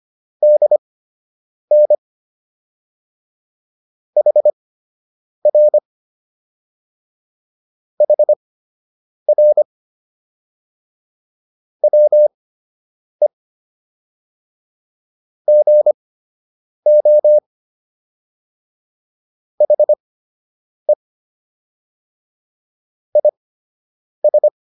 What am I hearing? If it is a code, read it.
Morse code, heard as a keyed tone, about 25 wpm, spaced as for 8 wpm, 600 Hz. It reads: DN HR HR WE GO HE IS